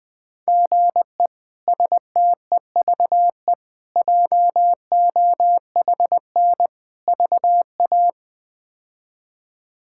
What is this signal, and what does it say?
Morse code, heard as a keyed tone, about 20 wpm, 700 Hz. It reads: GE STEVE JOHN VA
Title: K1USN SST QSO Practice - 20wpm